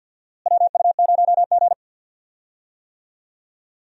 Morse code, heard as a keyed tone, about 53 wpm, 690 Hz. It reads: WU0G
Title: USA Call Signs - 4 Characters - Repeated 2x - 50wpm